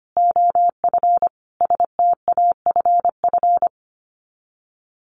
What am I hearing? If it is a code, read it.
Morse code, heard as a keyed tone, about 25 wpm, 700 Hz. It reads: OF STAFF
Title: Sets of 5 Words Encore2 25wpm